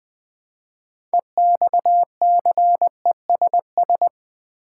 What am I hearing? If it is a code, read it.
Morse code, heard as a keyed tone, about 20 wpm, 700 Hz. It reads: EXCESS
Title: Build A Word - 3 Letters At A Time - 9-Letter Words - 20wpm